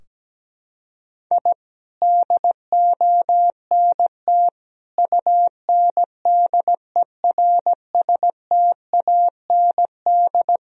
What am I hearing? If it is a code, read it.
Morse code, heard as a keyed tone, about 17 wpm, 700 Hz. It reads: I DONT UNDERSTAND
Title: Sentences from Top 1000 Words - Review Part 3 - 17wpm